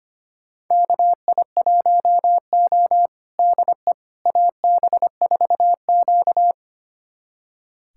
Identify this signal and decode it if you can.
Morse code, heard as a keyed tone, about 25 wpm, 700 Hz. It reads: KI1O DE AB4Q
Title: USA Callsign DE Callsign - 4 Characters - 25wpm